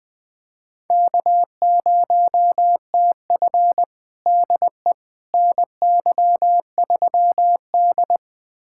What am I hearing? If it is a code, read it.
Morse code, heard as a keyed tone, about 20 wpm, 700 Hz. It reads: K0TF DE NY3D
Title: USA Callsign DE Callsign - 4 Characters - 20wpm